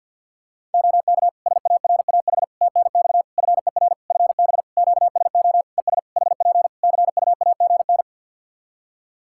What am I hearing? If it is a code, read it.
Morse code, heard as a keyed tone, about 50 wpm, 700 Hz. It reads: OK SARAH TNX FER FB QSO ES HP CUAGN
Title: Practice QSO - Exchange 4 - Ending - 50wpm